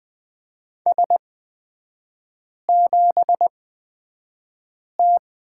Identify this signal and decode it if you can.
Morse code, heard as a keyed tone, about 20 wpm, spaced as for 5 wpm, 700 Hz. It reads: S7T
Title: 243 Lesson 39e - Letter z - Three Characters 20wpm